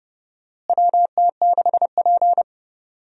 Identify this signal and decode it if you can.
Morse code, heard as a keyed tone, about 32 wpm, 700 Hz. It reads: WT6P
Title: USA Call Signs - 4 Characters - Repeated 2x - 30wpm